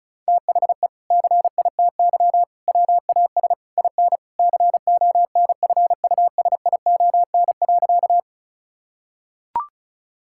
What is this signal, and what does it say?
Morse code, heard as a keyed tone, about 36 wpm, 700 Hz. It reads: THE CITY WAS IN CONFUSION.
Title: Sentences from Top 1600 Words 35wpm